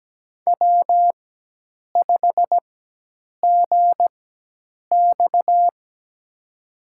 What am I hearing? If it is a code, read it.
Morse code, heard as a keyed tone, about 17 wpm, spaced as for 8 wpm, 700 Hz. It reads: W5GX
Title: USA Call Signs - 4 Characters - 4x Character Spacing - 17wpm